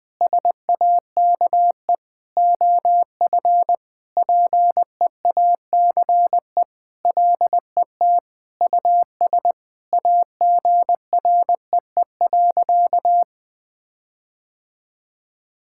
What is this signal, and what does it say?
Morse code, heard as a keyed tone, about 21 wpm, 700 Hz. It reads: SAKE OF PEACE LET US AGREE.
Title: Sentences with 4-Word Phrases - 20wpm